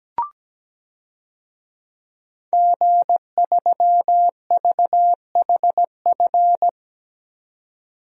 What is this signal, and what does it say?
Morse code, heard as a keyed tone, about 17 wpm, 700 Hz. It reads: G3VHF
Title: DXCC-100 Call Signs - 5 Characters - 17wpm